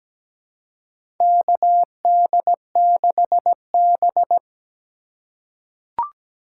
Morse code, heard as a keyed tone, about 17 wpm, 700 Hz. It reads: KD6B
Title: USA Call Signs - 4 Characters - 3x Character Spacing - 17wpm